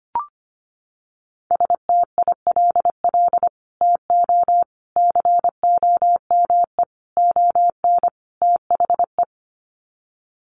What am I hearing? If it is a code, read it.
Morse code, heard as a keyed tone, about 25 wpm, 700 Hz. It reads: STILL TO COME ON THE
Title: Sets of 5 Words Encore2 25wpm